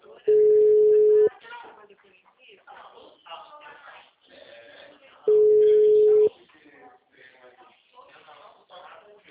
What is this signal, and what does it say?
Ringback tone (Europe, 425 Hz), with speech.
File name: out- -2006- - - .58321